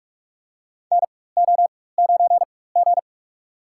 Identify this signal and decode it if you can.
Morse code, heard as a keyed tone, about 42 wpm, spaced as for 20 wpm, 700 Hz. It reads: NO9G
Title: USA Call Signs - 4 Characters - 4x Character Spacing - 45wpm